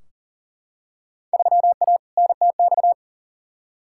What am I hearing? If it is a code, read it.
Morse code, heard as a keyed tone, about 39 wpm, 700 Hz. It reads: 3A NTX
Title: Field Day Exchanges - 40wpm